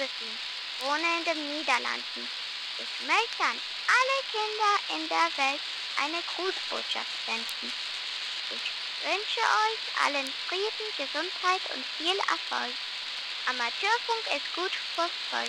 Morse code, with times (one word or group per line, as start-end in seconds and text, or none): none